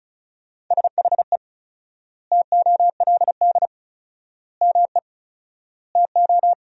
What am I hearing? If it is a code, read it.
Morse code, heard as a keyed tone, about 36 wpm, 700 Hz. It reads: SHE TOLD ME TO
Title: 4x Word-Spacing - Sentences from Top 400 Words - 35wpm